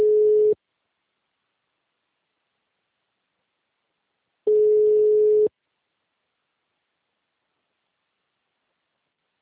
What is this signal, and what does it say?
Ringback tone (Europe, 425 Hz).